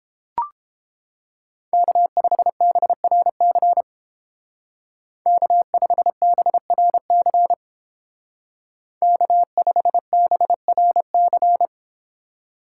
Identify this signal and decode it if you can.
Morse code, heard as a keyed tone, about 31 wpm, 700 Hz. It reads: K5BRC K5BRC K5BRC